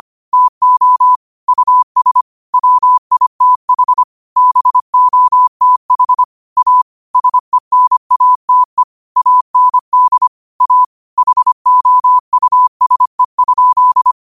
Morse code, heard as a keyed tone, about 25 wpm, 1000 Hz. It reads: TO US WITH BOTH A SENATE AND A HOUSE?